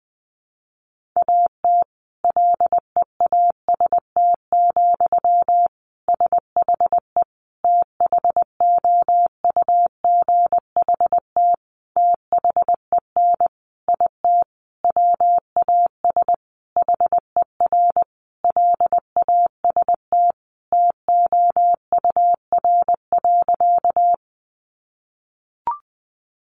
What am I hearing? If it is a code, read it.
Morse code, heard as a keyed tone, about 19 wpm, 700 Hz. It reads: AT LEAST, SHE THOUGHT THEN IT WAS HER LAST TOUR.
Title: Sentences from Top 1400 Words 20wpm